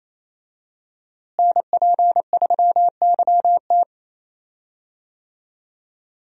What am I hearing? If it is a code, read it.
Morse code, heard as a keyed tone, about 28 wpm, 700 Hz. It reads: NP3YT
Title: USA Call Signs - 5 Characters - Repeated 2x - 28wpm